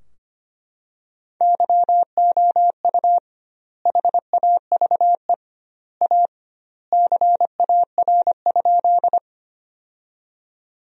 Morse code, heard as a keyed tone, about 25 wpm, 700 Hz. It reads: YOU HAVE A CAR?
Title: 2x Word-Spacing - Sentences from Top 300 Words - 25wpm